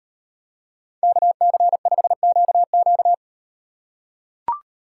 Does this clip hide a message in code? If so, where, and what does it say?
Yes: Morse code from 1.0 to 3.1 s, heard as a keyed tone, about 36 wpm, 700 Hz. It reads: KC5QQ